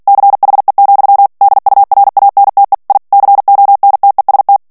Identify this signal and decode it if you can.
Morse code, heard as a keyed tone, about 48 wpm, 800 Hz. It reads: CHE, DURANTE I CONTEST
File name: ITALIEN texte clair 2 vitesse 45 mots minute